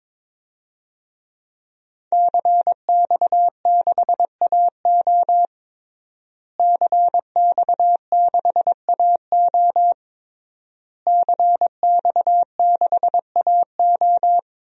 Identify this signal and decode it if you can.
Morse code, heard as a keyed tone, about 22 wpm, 700 Hz. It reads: CX6AO CX6AO CX6AO
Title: DXCC-100 Call Signs - 5 Characters - Repeated 3x - 22wpm